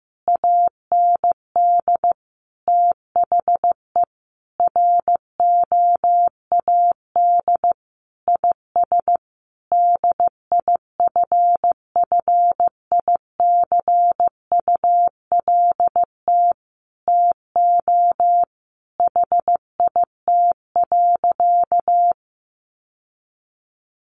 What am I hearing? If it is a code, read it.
Morse code, heard as a keyed tone, about 15 wpm, 700 Hz. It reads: AND THE ROAD IS DIFFICULT TO HIT.
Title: Sentences from Top 1200 Words 15wpm